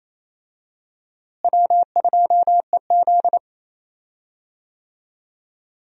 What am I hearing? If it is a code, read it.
Morse code, heard as a keyed tone, about 28 wpm, 700 Hz. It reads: W2EZ
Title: USA Call Signs - 4 Characters - Repeated 3x - 28wpm